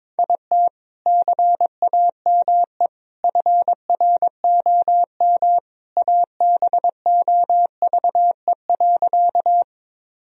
Morse code, heard as a keyed tone, about 22 wpm, 700 Hz. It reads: IT CAME FROM ABOVE.
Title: Sentences from Top 1000 Words - Review Part 2 - 22wpm